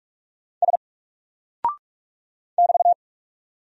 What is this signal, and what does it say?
Morse code, heard as a keyed tone, about 44 wpm, 700 Hz. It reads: S <BT>
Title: Single Character (Letter:Number:Punctuation:Prosign) 45wpm